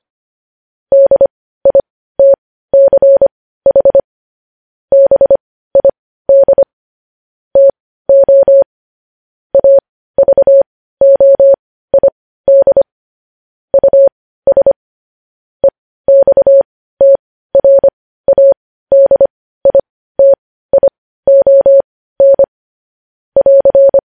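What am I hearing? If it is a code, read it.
Morse code, heard as a keyed tone, about 25 wpm, spaced as for 15 wpm, 550 Hz. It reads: DITCH BID TO AVOID US EXTRADITION <AR>
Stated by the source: Morse news headlines practice at 15.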